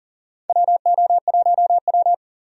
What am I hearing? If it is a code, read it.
Morse code, heard as a keyed tone, about 40 wpm, 690 Hz. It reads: WO1W
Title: USA Call Signs - 4 Characters - 40wpm